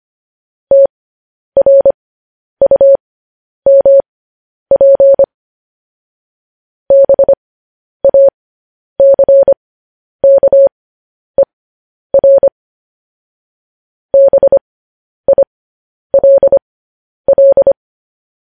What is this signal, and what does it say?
Morse code, heard as a keyed tone, about 25 wpm, spaced as for 10 wpm, 550 Hz. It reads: TRUMP BACKER BILL
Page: news headlines in morse code at 10 wpm